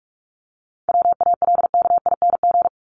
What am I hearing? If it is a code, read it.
Morse code, heard as a keyed tone, about 44 wpm, 700 Hz. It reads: WALKING
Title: ING Suffix - Top 100 Words with Base Words 45wpm